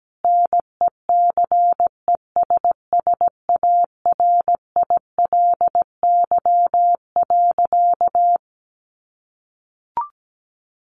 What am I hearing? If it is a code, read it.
Morse code, heard as a keyed tone, about 17 wpm, 700 Hz. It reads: NECESSARILY.
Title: Sentences from Top 1400 Words 17wpm